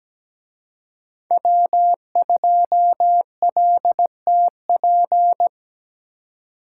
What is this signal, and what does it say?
Morse code, heard as a keyed tone, about 20 wpm, 700 Hz. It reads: W2LTP
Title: Speed Racing - USA Call Signs 17wpm